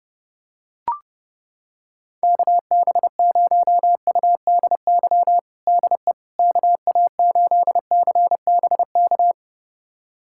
Morse code, heard as a keyed tone, about 30 wpm, 700 Hz. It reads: KB0UDY DE KA8CBK
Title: USA Callsign DE Callsign - 6 Characters - 30wpm